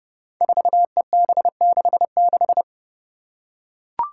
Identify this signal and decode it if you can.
Morse code, heard as a keyed tone, about 29 wpm, 700 Hz. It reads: 4EB66
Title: Five Challenging Letter-Number Combinations 30wpm